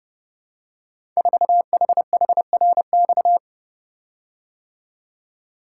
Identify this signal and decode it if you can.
Morse code, heard as a keyed tone, about 30 wpm, 700 Hz. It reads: 4HHRX
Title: Code Groups - 5 Letter:Numbers - 30wpm